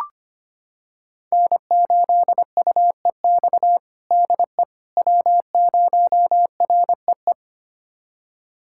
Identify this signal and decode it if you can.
Morse code, heard as a keyed tone, about 25 wpm, 700 Hz. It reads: N8UEX DE W0REE
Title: USA Callsign DE Callsign - 5 Characters - 25wpm